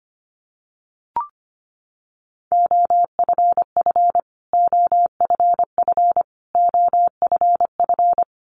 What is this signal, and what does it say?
Morse code, heard as a keyed tone, about 25 wpm, 700 Hz. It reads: OFF OFF OFF
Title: Top 500 Words Repeated 25wpm